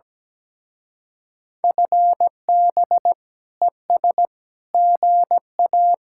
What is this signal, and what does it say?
Morse code, heard as a keyed tone, about 18 wpm, 700 Hz. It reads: FB ES GA
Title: Speed-Racing Common QSO Phrases Encore - 17wpm